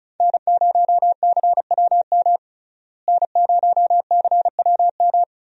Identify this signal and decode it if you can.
Morse code, heard as a keyed tone, about 34 wpm, 700 Hz. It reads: N0CWM N0CWM
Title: USA Call Signs - 5 Characters - Repeated 3x - 35wpm